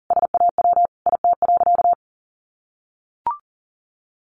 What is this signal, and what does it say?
Morse code, heard as a keyed tone, about 37 wpm, 700 Hz. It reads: SAW IT.
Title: Sentences from Top 1500 Words 40wpm